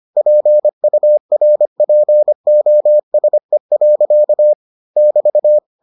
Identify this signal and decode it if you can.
Morse code, heard as a keyed tone, about 25 wpm, 600 Hz. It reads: PURPOSE. <BT>